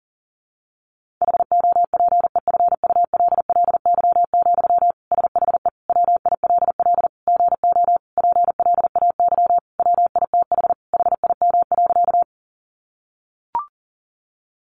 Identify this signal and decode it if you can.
Morse code, heard as a keyed tone, about 40 wpm, 700 Hz. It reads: HOPEFULLY, SHE WILL GO PLAY WITH HIM.
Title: Sentences from Top 1500 Words 40wpm